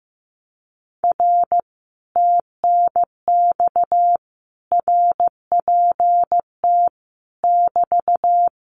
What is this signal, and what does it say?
Morse code, heard as a keyed tone, about 15 wpm, 700 Hz. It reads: R TNX RPT <BT>
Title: Common QSO Phrases Encore 15wpm